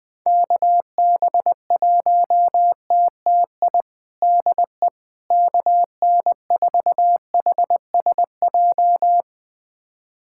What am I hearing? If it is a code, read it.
Morse code, heard as a keyed tone, about 20 wpm, 700 Hz. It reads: KB1TTI DE KN4HSJ
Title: USA Callsign DE Callsign 20wpm